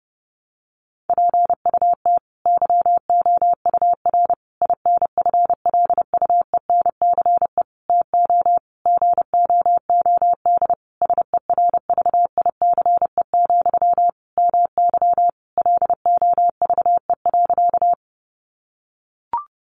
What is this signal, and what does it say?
Morse code, heard as a keyed tone, about 30 wpm, 700 Hz. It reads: PUT YOUR INFLUENCE TO GOOD SERVICE, MY LOVE.
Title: Sentences from Top 1100 Words 30wpm